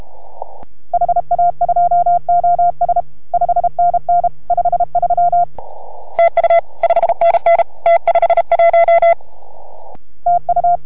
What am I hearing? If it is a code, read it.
Morse code, heard as a keyed tone, about 33 wpm, 690 Hz. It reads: HA2OS 5NN 53 TU 5NN T51 TU